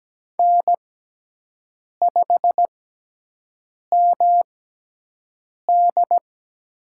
Morse code, heard as a keyed tone, about 17 wpm, spaced as for 6 wpm, 700 Hz. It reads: N5MD
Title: USA Call Signs - 6x Character Spacing - 17wpm